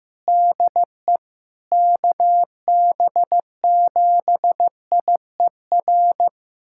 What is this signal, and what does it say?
Morse code, heard as a keyed tone, about 16 wpm, 700 Hz. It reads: DE KB7IER
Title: USA Callsign DE Callsign - 6 Characters - 15wpm